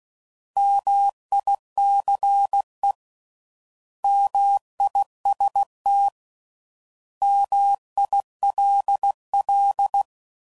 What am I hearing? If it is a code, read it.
Morse code, heard as a keyed tone, about 16 wpm, 800 Hz. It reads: MICE MIST MILL